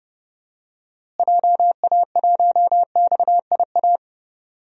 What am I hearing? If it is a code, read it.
Morse code, heard as a keyed tone, about 30 wpm, 700 Hz. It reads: JA1XIA
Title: DXCC-100 Call Signs - 6 Characters - Repeated 2x - 30wpm